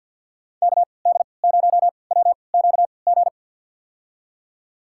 Morse code, heard as a keyed tone, about 48 wpm, spaced as for 28 wpm, 700 Hz. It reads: KD0WQG
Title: USA Call Signs - 3x Character Spacing - 50wpm